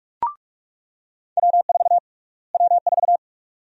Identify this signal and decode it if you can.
Morse code, heard as a keyed tone, about 42 wpm, 700 Hz. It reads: W4 W4
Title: USA Call Sign Prefixes - Repeated 2x - 45wpm